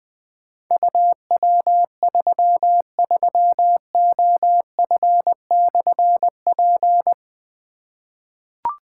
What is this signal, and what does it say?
Morse code, heard as a keyed tone, about 20 wpm, 700 Hz. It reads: UW33OF/P
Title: Difficult Call Signs 20wpm